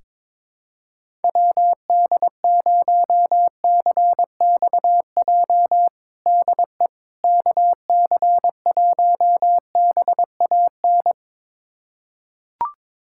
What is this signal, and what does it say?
Morse code, heard as a keyed tone, about 22 wpm, 700 Hz. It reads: WD0CXJ DE KC1BAN
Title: USA Callsign DE Callsign - 6 Characters - 22wpm